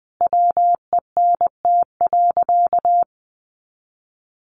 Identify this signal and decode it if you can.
Morse code, heard as a keyed tone, about 20 wpm, 700 Hz. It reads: WENT.
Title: Sentences from Top 1400 Words 20wpm